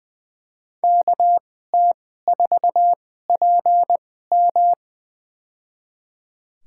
Morse code, heard as a keyed tone, about 20 wpm, spaced as for 15 wpm, 700 Hz. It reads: KT4PM
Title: USA Call Signs - 2x Character Spacing - 20wpm